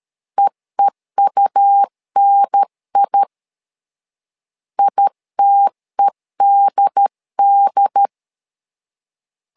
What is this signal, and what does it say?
Morse code, heard as a keyed tone, about 13 wpm, 800 Hz. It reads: EEUNI ITEDD
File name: LEZIONI / 2 UHDN / 2 UHDN 12wpm 1456